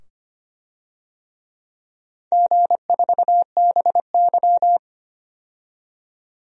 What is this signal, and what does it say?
Morse code, heard as a keyed tone, about 25 wpm, 700 Hz. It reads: G4BY
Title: DXCC-100 Call Signs - 4 Characters - 25wpm